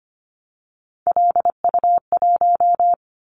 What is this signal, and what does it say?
Morse code, heard as a keyed tone, about 25 wpm, 700 Hz. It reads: LU1
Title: Callsign Prefixes for Top 100 DXCC - 25wpm